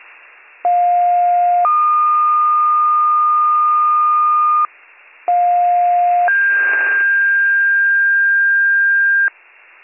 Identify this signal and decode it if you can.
Two-tone page, heard as a one-second tone, then a three-second tone, about 710 then 1220 Hz, about 710 then 1600 Hz, with noise. It is heard from 0.6 to 9.3 s.